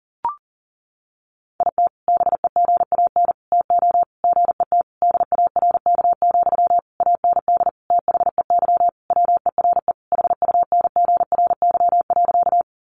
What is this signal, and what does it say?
Morse code, heard as a keyed tone, about 40 wpm, 700 Hz. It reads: IT BEGAN TO GET DARK, AND THEY WERE HUNGRY.
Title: Sentences from Top 1400 Words 40wpm